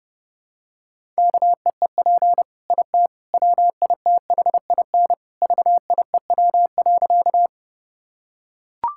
Morse code, heard as a keyed tone, about 30 wpm, 700 Hz. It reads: KEEP IT WITHIN VIEW.
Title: Sentences from Top 1000 Words - Review Part 6 - 30wpm